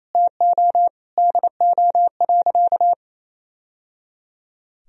Morse code, heard as a keyed tone, about 28 wpm, 700 Hz. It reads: TO DO.
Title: Sentences from Top 1000 Words - Review Part 4 - 28wpm